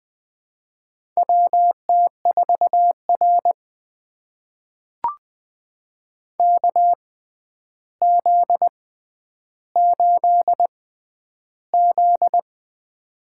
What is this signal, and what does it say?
Morse code, heard as a keyed tone, about 20 wpm, spaced as for 10 wpm, 700 Hz. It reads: WT4R KZ8Z
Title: USA Call Signs - 4 Characters - 6x Character Spacing - 20wpm